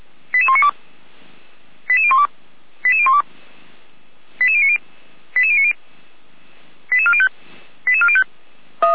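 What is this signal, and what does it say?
Five-tone selective call (ZVEI), heard as quick runs of five tones, tones of about 70 ms. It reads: 88192 88012 88012 88090 88090 88405 88405
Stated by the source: ZVEI five-tone call 88102 received by SDR, archived by Feuerwehr Orlamünde fire brigade.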